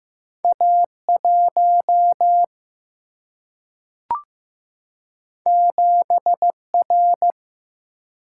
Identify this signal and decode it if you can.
Morse code, heard as a keyed tone, about 15 wpm, 700 Hz. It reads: A 1 7 R
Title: Single-Letter Single-Number 15wpm